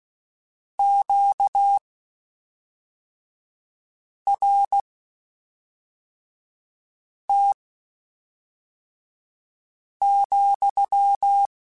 Morse code, heard as a keyed tone, about 16 wpm, spaced as for 3 wpm, 800 Hz. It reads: QRT,
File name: Q Calls / Q Calls-mp3s / Q Calls sppc @5